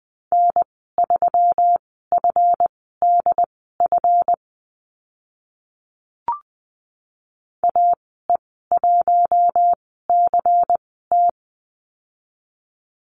Morse code, heard as a keyed tone, about 20 wpm, spaced as for 14 wpm, 700 Hz. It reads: N3FDF AE1CT
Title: 140 Lesson 23g - Letter f - Call Signs 20wpm